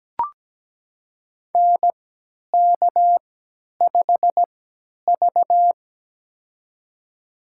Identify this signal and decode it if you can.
Morse code, heard as a keyed tone, about 17 wpm, spaced as for 10 wpm, 700 Hz. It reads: NK5V
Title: USA Call Signs - 4 Characters - 3x Character Spacing - 17wpm